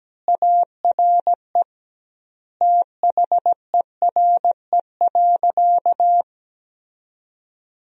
Morse code, heard as a keyed tone, about 17 wpm, 700 Hz. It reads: ARE THERE.
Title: 2x Word-Spacing - Sentences from Top 500 Words - 17wpm